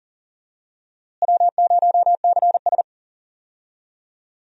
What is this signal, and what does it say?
Morse code, heard as a keyed tone, about 38 wpm, 700 Hz. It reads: W0CS
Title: USA Call Signs - 4 Characters - Repeated 2x - 40wpm